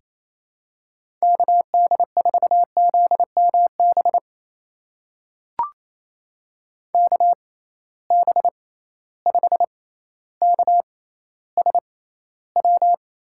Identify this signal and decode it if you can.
Morse code, heard as a keyed tone, about 28 wpm, spaced as for 14 wpm, 700 Hz. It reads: KD4ZMB KB5KSW
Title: USA Call Signs - 6 Characters - 6x Character Spacing - 28wpm